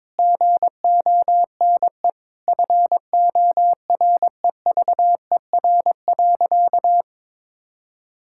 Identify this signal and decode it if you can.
Morse code, heard as a keyed tone, about 22 wpm, 700 Hz. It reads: GONE FOREVER.
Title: Sentences from Top 1600 Words 22wpm